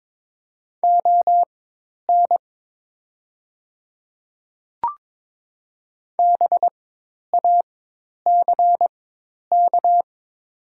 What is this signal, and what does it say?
Morse code, heard as a keyed tone, about 22 wpm, spaced as for 10 wpm, 700 Hz. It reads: ON BACK
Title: 4x Character Spacing - Top 300 Words - 22wpm